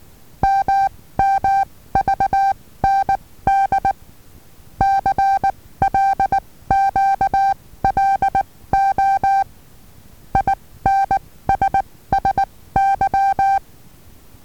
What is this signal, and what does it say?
Morse code, heard as a keyed tone, about 19 wpm, spaced as for 15 wpm, 780 Hz. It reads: MMVND CLQLO INSSY
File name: Taviro HA6VA B